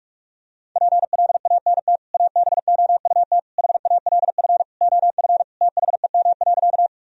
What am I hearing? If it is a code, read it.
Morse code, heard as a keyed tone, about 48 wpm, 700 Hz. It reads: PLANT ABOUT HALF OF THEM.
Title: Sentences from Top 1000 Words - Review Part 1 - 45wpm